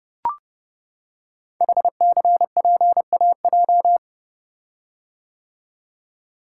Morse code, heard as a keyed tone, about 30 wpm, 690 Hz. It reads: HCPAJ
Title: Code Groups - 5 Letters - 30wpm